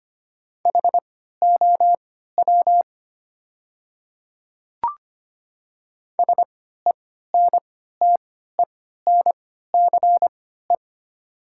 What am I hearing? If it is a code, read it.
Morse code, heard as a keyed tone, about 25 wpm, spaced as for 14 wpm, 700 Hz. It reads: HOW SENTENCE
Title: 3x Character Spacing - Top 200 Words - 25wpm